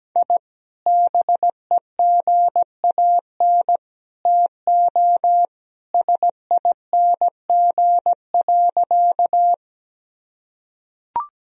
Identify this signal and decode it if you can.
Morse code, heard as a keyed tone, about 17 wpm, 700 Hz. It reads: I BEGAN TO SING.
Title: Sentences from Top 1000 Words - Review Part 2 - 17wpm